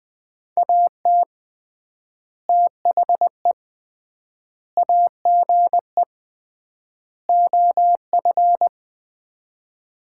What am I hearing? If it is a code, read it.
Morse code, heard as a keyed tone, about 21 wpm, 700 Hz. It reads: AT THE AGE OF
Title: 3x Word-Spacing - Sets of 4 Words - 20wpm